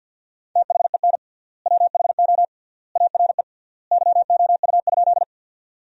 Morse code, heard as a keyed tone, about 50 wpm, 700 Hz. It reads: THEN WHO ARE YOU?
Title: 3x Word-Spacing - Sentences from Top 100 Words - 50wpm